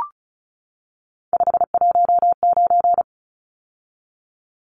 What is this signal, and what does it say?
Morse code, heard as a keyed tone, about 34 wpm, 700 Hz. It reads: 519
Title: Numbers - 3-Digit 35wpm